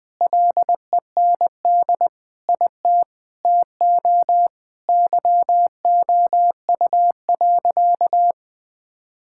Morse code, heard as a keyed tone, about 20 wpm, 700 Hz. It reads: LEND IT TO YOU.
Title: Sentences with 5-Word Phrases - 20wpm